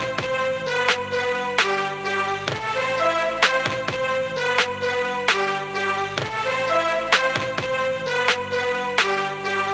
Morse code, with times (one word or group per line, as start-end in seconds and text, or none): none